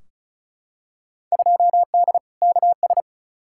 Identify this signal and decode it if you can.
Morse code, heard as a keyed tone, about 35 wpm, 700 Hz. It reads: 2D KS
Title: Field Day Exchanges Encore - 35wpm